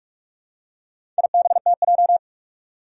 Morse code, heard as a keyed tone, about 42 wpm, 700 Hz. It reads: IBTJ
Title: Four Challenging Letter-Number Combinations 45wpm